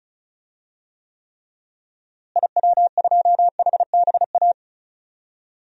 Morse code, heard as a keyed tone, about 36 wpm, 700 Hz. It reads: IW2HBA
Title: DXCC-100 Call Signs - 6 Characters - Repeated 3x - 35wpm